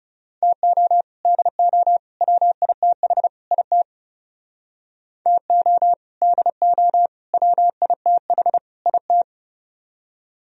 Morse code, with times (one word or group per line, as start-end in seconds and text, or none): 0.4-1.0 TO
1.2-2.0 DO
2.2-3.3 WITH
3.5-3.8 IT
5.3-5.9 TO
6.2-7.1 DO
7.3-8.6 WITH
8.9-9.2 IT